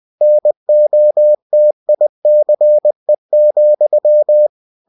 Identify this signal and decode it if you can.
Morse code, heard as a keyed tone, about 20 wpm, 600 Hz. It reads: NOTICE,
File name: winnie-20wpm